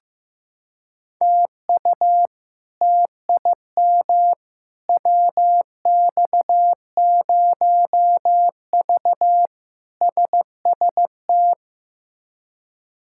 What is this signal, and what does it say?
Morse code, heard as a keyed tone, about 15 wpm, 700 Hz. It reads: TU TIM WX0V SST
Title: K1USN SST QSO Practice - 15wpm